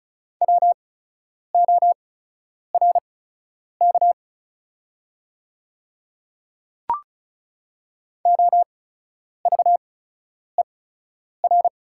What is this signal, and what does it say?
Morse code, heard as a keyed tone, about 36 wpm, spaced as for 10 wpm, 700 Hz. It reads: WORK OVER
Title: 8x Character Spacing - Top 200 Words - 35wpm